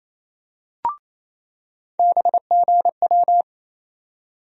Morse code, heard as a keyed tone, about 28 wpm, 700 Hz. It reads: BGW
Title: USA Call Sign Postfixes - 3 Letters 28wpm